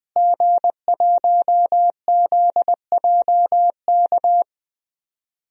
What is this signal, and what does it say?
Morse code, heard as a keyed tone, about 20 wpm, 700 Hz. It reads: G1ZJK
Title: DXCC-100 Call Signs - 5 Characters - Repeated 3x - 20wpm